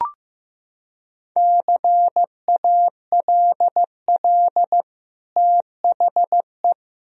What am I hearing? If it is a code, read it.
Morse code, heard as a keyed tone, about 15 wpm, 700 Hz. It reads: CALL THE
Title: Sentences from Top 1000 Words - Review Part 4 - 15wpm